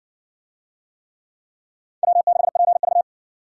Morse code, heard as a keyed tone, about 55 wpm, 700 Hz. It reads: W6PV